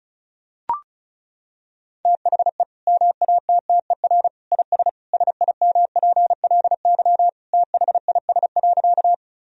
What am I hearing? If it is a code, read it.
Morse code, heard as a keyed tone, about 36 wpm, 700 Hz. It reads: THE MATTER IS SIMPLY THIS.
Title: Sentences from Top 1000 Words - Review Part 4 - 35wpm